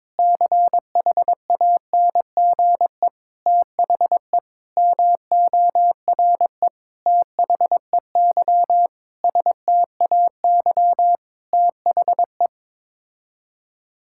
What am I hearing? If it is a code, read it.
Morse code, heard as a keyed tone, about 22 wpm, 700 Hz. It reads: CHANGE THE MORE THEY STAY THE
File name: Sets of 9 Words - 22wpm